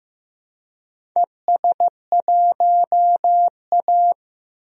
Morse code, heard as a keyed tone, about 15 wpm, 700 Hz. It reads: ES1A
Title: DXCC-100 Call Signs - 4 Characters - 15wpm